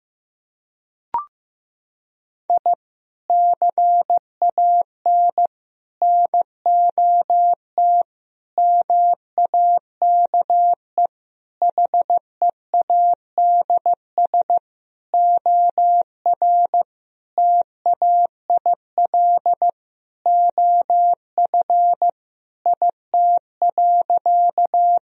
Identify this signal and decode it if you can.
Morse code, heard as a keyed tone, about 15 wpm, 700 Hz. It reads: I CAN NOT MAKE HEADS OR TAIL OF IT.
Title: Sentences from Top 1000 Words - Review Part 3 - 15wpm